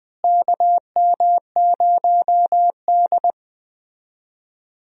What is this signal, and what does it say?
Morse code, heard as a keyed tone, about 20 wpm, 700 Hz. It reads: KM0D
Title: USA Call Signs - 4 Characters - Repeated 3x - 20wpm